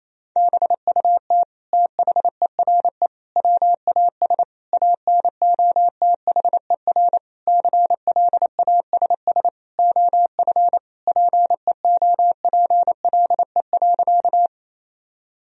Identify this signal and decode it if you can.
Morse code, heard as a keyed tone, about 28 wpm, 700 Hz. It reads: BUT THERE WAS ANOTHER CLASS OF PEOPLE.
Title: Sentences from Top 1000 Words - Review Part 2 - 28wpm